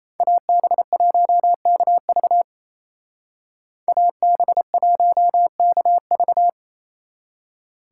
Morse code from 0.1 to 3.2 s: AB1KV

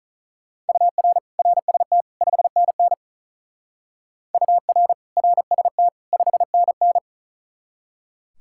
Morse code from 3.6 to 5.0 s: UR

